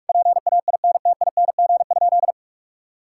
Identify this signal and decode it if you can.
Morse code, heard as a keyed tone, about 46 wpm, 700 Hz. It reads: PAINTING?
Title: Sentences from Top 1000 Words - Review Part 6 - 45wpm